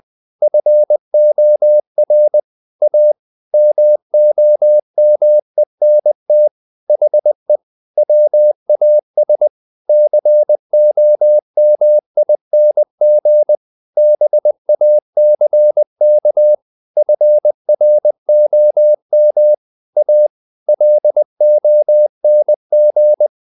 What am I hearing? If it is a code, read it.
Morse code, heard as a keyed tone, about 20 wpm, 600 Hz. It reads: FOR A MOMENT HE WAS COMING BACK FROM A LONG
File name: the old man and the sea / the old man and the sea part 1 20wpm